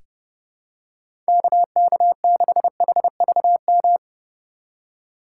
Morse code, heard as a keyed tone, about 30 wpm, 700 Hz. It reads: KK6HVM
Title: USA Call Signs - 6 Characters - 3x Character Spacing - 30wpm